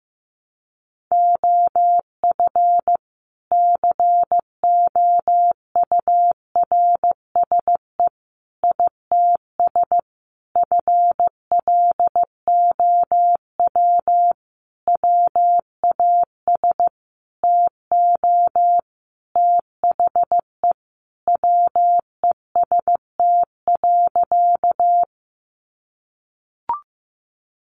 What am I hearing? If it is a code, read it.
Morse code, heard as a keyed tone, about 15 wpm, 700 Hz. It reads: OF COURSE ITS FLOW WAS TO THE WEST.
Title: Sentences from Top 1200 Words 15wpm